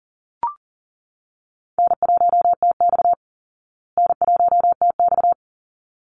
Morse code, heard as a keyed tone, about 40 wpm, 700 Hz. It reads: N1TX N1TX
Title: Call Signs Repeated 3x - 40wpm